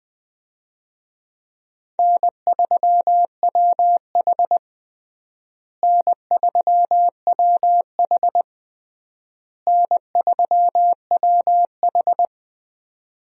Morse code, heard as a keyed tone, about 21 wpm, 700 Hz. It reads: N3WH N3WH N3WH
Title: USA Call Signs - 4 Characters - Repeated 3x - 20wpm